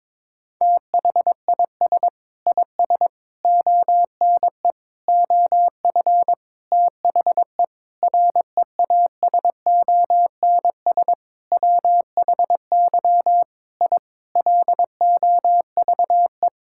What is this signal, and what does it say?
Morse code, heard as a keyed tone, about 21 wpm, 700 Hz. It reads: THIS IS ONE OF THE REASONS WHY I LOVE
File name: Sentences with 7-Word Phrases - 22wpm